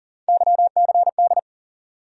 Morse code, heard as a keyed tone, about 37 wpm, 700 Hz. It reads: YCD
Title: USA Call Sign Postfixes - 3 Letters 40wpm